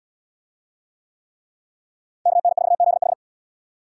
Morse code, heard as a keyed tone, about 68 wpm, 700 Hz. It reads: KN4ZH